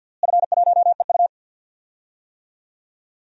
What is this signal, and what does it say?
Morse code, heard as a keyed tone, about 50 wpm, 700 Hz. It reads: F1EU
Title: DXCC-100 Call Signs - 4 Characters - Repeated 2x - 50wpm